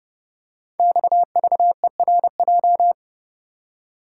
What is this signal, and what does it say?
Morse code, heard as a keyed tone, about 30 wpm, 700 Hz. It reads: XVERJ
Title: Code Groups - 5 Letters - 30wpm